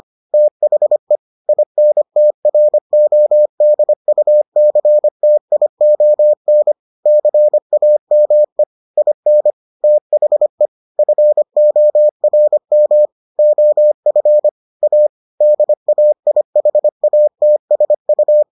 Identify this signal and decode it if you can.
Morse code, heard as a keyed tone, about 25 wpm, 600 Hz. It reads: THE INTRODUCTION CAME IN THE FORM OF A DAIHATSU